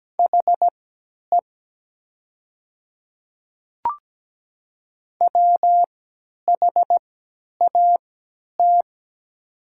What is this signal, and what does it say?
Morse code, heard as a keyed tone, about 17 wpm, spaced as for 10 wpm, 700 Hz. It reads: HE WHAT
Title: 3x Character Spacing - Top 100 Words - 17wpm